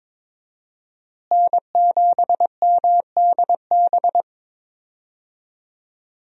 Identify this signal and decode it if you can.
Morse code, heard as a keyed tone, about 21 wpm, 700 Hz. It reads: N7MDB
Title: USA Call Signs - 5 Characters - Repeated 3x - 22wpm